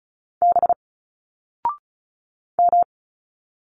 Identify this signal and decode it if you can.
Morse code, heard as a keyed tone, about 36 wpm, 700 Hz. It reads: B M
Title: Single Letters 35wpm